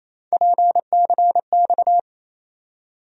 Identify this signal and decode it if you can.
Morse code, heard as a keyed tone, about 28 wpm, 700 Hz. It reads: PCX
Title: USA Call Sign Postfixes - 3 Letters 28wpm